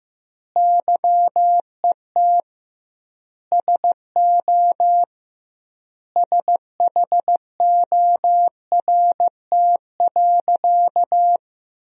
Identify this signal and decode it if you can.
Morse code, heard as a keyed tone, about 16 wpm, 700 Hz. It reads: YET SO SHORT.
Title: 2x Word-Spacing - Sentences from Top 500 Words - 15wpm